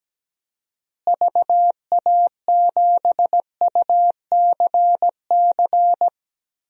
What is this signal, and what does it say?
Morse code, heard as a keyed tone, about 17 wpm, 700 Hz. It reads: VA7UCC
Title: DXCC-100 Call Signs - 6 Characters - Repeated 2x - 17wpm